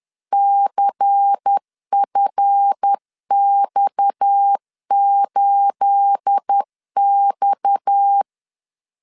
Morse code, heard as a keyed tone, about 10 wpm, 800 Hz. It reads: CFX8X
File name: CW / DET / 1 9 1500C S A 10wpm 4511